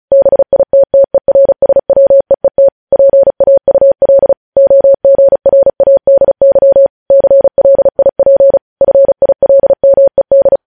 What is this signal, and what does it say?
Morse code, heard as a keyed tone, about 36 wpm, 550 Hz. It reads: BITTERSWEET PAUL OGRADY CLIP FILMED
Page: News headlines in Morse code at 35 wpm